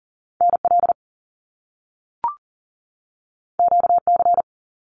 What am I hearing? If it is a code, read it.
Morse code, heard as a keyed tone, about 41 wpm, 700 Hz. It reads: NL QC
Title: Canadian Provinces and Territories 40wpm